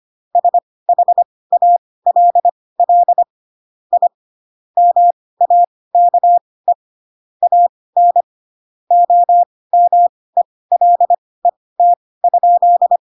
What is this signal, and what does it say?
Morse code, heard as a keyed tone, about 25 wpm, spaced as for 18 wpm, 700 Hz. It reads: SHALL I MAKE AN OMELET?